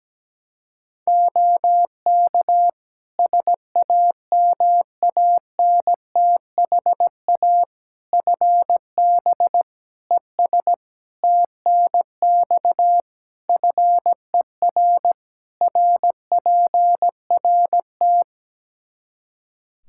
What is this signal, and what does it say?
Morse code, heard as a keyed tone, about 17 wpm, 700 Hz. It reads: OK SAMANTHA FB ES TNX FER RPRT
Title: Practice QSO - Exchange 2 - Info - 17wpm